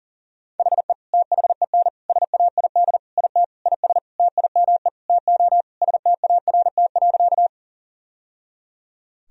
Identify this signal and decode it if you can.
Morse code, heard as a keyed tone, about 40 wpm, 700 Hz. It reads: HE THEN SAID IT IS TIME TO START.
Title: Sentences from Top 1000 Words - Review Part 1 - 40wpm